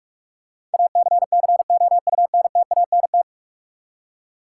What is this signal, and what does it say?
Morse code, heard as a keyed tone, about 41 wpm, 700 Hz. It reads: ACCOUNTANT
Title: Practice QSO - Exchange 3 - More Info - 45wpm